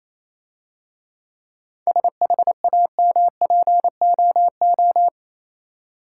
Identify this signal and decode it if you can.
Morse code, heard as a keyed tone, about 27 wpm, 690 Hz. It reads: SHAMPOO
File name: Easy Words - Speed-Racing with Hints - 22wpm